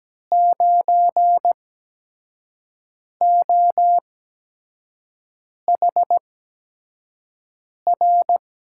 Morse code, heard as a keyed tone, about 17 wpm, spaced as for 5 wpm, 700 Hz. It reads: 9OHR
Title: USA Call Signs - 6 Characters - 8x Character Spacing - 17wpm